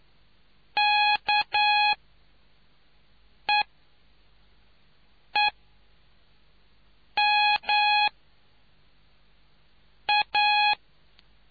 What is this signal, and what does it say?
Morse code, heard as a keyed tone, about 9 wpm, spaced as for 4 wpm, 830 Hz. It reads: KEEM A